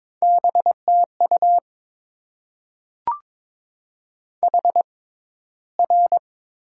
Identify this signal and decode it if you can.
Morse code, heard as a keyed tone, about 22 wpm, spaced as for 13 wpm, 700 Hz. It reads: BTU HR
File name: Top 90 QSO Elements - 6x Character Spacing - 22wpm